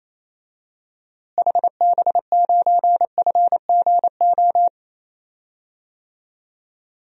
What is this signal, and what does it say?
Morse code, heard as a keyed tone, about 27 wpm, 700 Hz. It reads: HB9FGO
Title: DXCC-100 Call Signs - 6 Characters - Repeated 2x - 28wpm